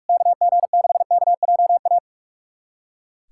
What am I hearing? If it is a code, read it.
Morse code, heard as a keyed tone, about 46 wpm, 690 Hz. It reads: KG6KJA
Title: USA Callsign DE Callsign - 6 Characters - 45wpm